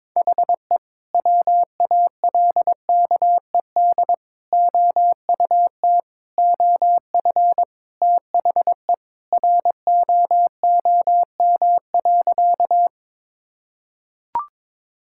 Morse code, heard as a keyed tone, about 22 wpm, 700 Hz. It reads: HE WALKED OUT OF THE ROOM.
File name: practice / 22wpm / Sentences with 2-Word Phrases - 22wpm